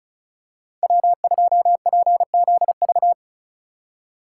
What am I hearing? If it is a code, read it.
Morse code, heard as a keyed tone, about 36 wpm, 700 Hz. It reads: W2PZV
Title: USA Call Signs - 5 Characters - 2x Character Spacing - 35wpm